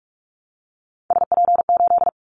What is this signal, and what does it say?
Morse code, heard as a keyed tone, about 46 wpm, 700 Hz. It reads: SP8
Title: Callsign Prefixes for Top 25 DXCC - 45wpm